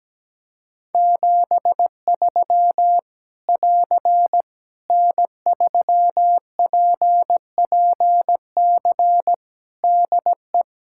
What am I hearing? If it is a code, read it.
Morse code, heard as a keyed tone, about 17 wpm, 700 Hz. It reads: 73 <AR> N3PPC DE
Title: K1USN SST QSO Practice 17wpm